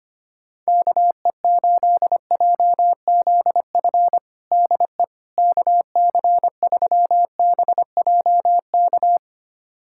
Morse code, heard as a keyed tone, about 25 wpm, 700 Hz. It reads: KE8JZF DE KC3BJK
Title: USA Callsign DE Callsign - 6 Characters - 25wpm